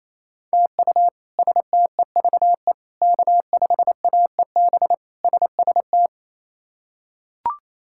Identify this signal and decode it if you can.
Morse code, heard as a keyed tone, about 29 wpm, 700 Hz. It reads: TU STEVE K5AEB SST
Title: K1USN SST QSO Practice - 28wpm